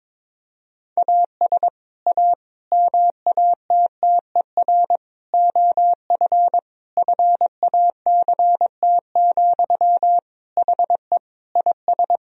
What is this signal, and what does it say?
Morse code, heard as a keyed tone, about 21 wpm, 700 Hz. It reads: AS A MATTER OF FACT, HE IS
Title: Sentences from Top 1000 Words - Review Part 3 - 22wpm